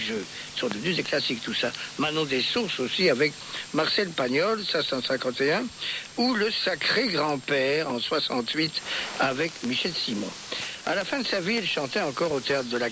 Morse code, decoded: none